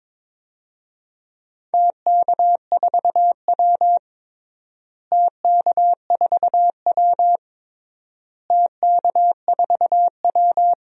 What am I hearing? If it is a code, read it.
Morse code, heard as a keyed tone, about 22 wpm, 700 Hz. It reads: TK4W TK4W TK4W
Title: DXCC-100 Call Signs - 4 Characters - Repeated 3x - 22wpm